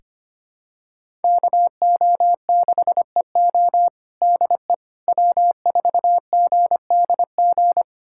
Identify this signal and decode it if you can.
Morse code, heard as a keyed tone, about 25 wpm, 700 Hz. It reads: KO6EO DE W4GDG
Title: USA Callsign DE Callsign - 5 Characters - 25wpm